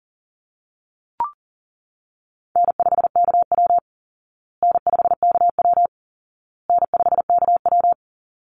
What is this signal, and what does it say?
Morse code, heard as a keyed tone, about 40 wpm, 700 Hz. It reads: N5KW N5KW N5KW
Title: Easy Call Signs Repeated 3x - 40wpm